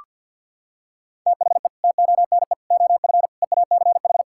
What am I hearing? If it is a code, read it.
Morse code, heard as a keyed tone, about 46 wpm, 700 Hz. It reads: THE TONE OF EACH
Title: Sentences from Top 1600 Words 50wpm